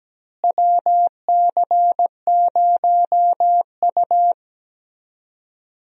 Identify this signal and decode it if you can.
Morse code, heard as a keyed tone, about 17 wpm, 700 Hz. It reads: WC0U
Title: USA Call Signs - 4 Characters - Repeated 2x - 17wpm